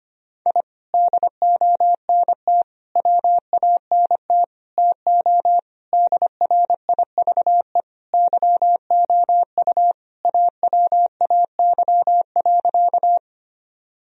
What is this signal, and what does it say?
Morse code, heard as a keyed tone, about 25 wpm, 700 Hz. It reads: I DONT WANT TO DRIVE YOU AWAY.
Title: Sentences from Top 1000 Words - Review Part 2 - 25wpm